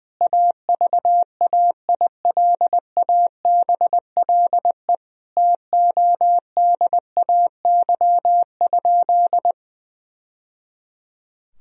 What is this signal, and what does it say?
Morse code, heard as a keyed tone, about 20 wpm, 700 Hz. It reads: AVAILABLE TODAY?
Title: Sentences from Top 1000 Words - Review Part 3 - 20wpm